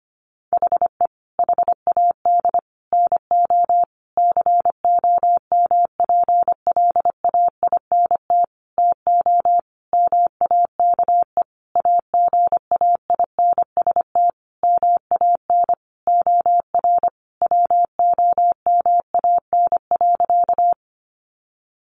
Morse code, heard as a keyed tone, about 25 wpm, 700 Hz. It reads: HE HAD NO COMPLAINT TO MAKE AGAINST MAN OR WOMAN.
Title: Sentences from Top 1400 Words 25wpm